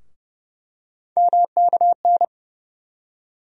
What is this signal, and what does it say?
Morse code, heard as a keyed tone, about 31 wpm, 700 Hz. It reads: MKN
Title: USA Call Sign Postfixes - 3 Letters 30wpm